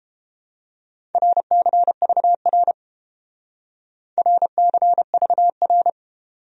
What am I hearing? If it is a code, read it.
Morse code, heard as a keyed tone, about 30 wpm, 700 Hz. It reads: RCVR RCVR